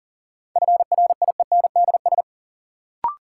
Morse code, heard as a keyed tone, about 40 wpm, 700 Hz. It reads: FRIENDS
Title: Sentences from Top 1000 Words - Review Part 5 - New Word Drill - 40wpm